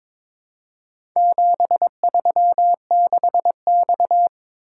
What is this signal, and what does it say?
Morse code, heard as a keyed tone, about 22 wpm, 700 Hz. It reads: 736X
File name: practice / 22wpm / Four Letter-Number Combinations - 22wpm